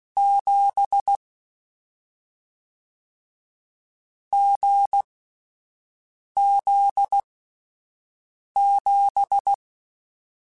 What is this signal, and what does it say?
Morse code, heard as a keyed tone, about 16 wpm, spaced as for 5 wpm, 800 Hz. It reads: 7 GZ7
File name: Difficult Combos / Difficult Combos-mp3s / awj1 ndb6 gz7 @5